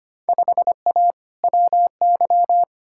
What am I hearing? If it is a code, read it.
Morse code, heard as a keyed tone, about 25 wpm, 700 Hz. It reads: 5A WY
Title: Field Day Exchanges Encore - 25wpm